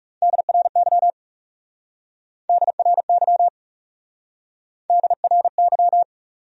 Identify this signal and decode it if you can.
Morse code, heard as a keyed tone, about 39 wpm, 700 Hz. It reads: DRY DRY DRY